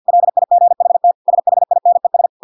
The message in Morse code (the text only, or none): LIGHT SHINES